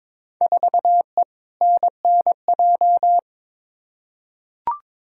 Morse code, heard as a keyed tone, about 22 wpm, 700 Hz. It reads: 4E NNJ
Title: Field Day Exchanges - 22wpm